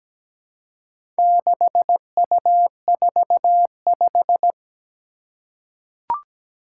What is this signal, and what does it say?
Morse code, heard as a keyed tone, about 17 wpm, 700 Hz. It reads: 6U45
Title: Four Challenging Letter-Number Combinations 17wpm